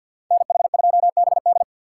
Morse code, heard as a keyed tone, about 51 wpm, 700 Hz. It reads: NH2BD
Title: USA Call Signs - 6x Character Spacing - 50wpm